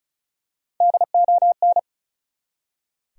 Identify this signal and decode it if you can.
Morse code, heard as a keyed tone, about 34 wpm, 700 Hz. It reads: DON
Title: USA Common Names - Max 4 Letters - 35wpm